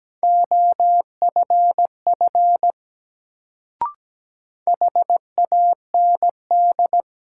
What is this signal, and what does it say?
Morse code, heard as a keyed tone, about 17 wpm, 700 Hz. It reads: OFF HAND
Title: Sentences from Top 200 Words - New Word Drill - 17wpm